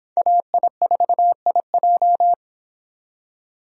Morse code, heard as a keyed tone, about 25 wpm, 700 Hz. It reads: AI4IJ